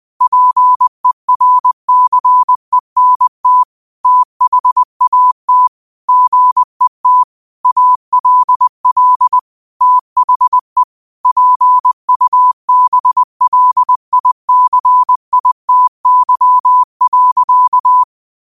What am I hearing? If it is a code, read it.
Morse code, heard as a keyed tone, about 20 wpm, 1000 Hz. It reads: PERCENT THAT GET ALL THE PUBLICITY.